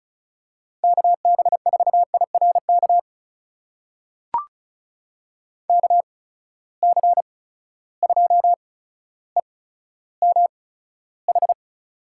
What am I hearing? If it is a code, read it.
Morse code, heard as a keyed tone, about 36 wpm, spaced as for 15 wpm, 700 Hz. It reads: KB4IRK KC2EMH
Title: USA Call Signs - 6 Characters - 8x Character Spacing - 35wpm